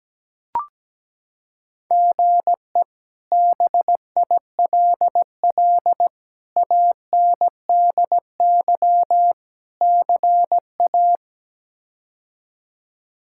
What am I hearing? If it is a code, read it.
Morse code, heard as a keyed tone, about 17 wpm, 700 Hz. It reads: GE BILL ANDY CA
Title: K1USN SST QSO Practice - 17wpm